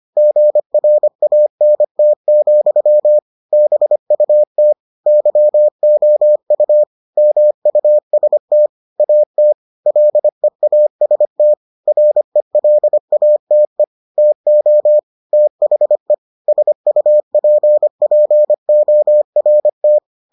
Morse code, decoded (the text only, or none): GRANT, BUT YOU MUST AT LEAST RELATE TO THE SUPPORT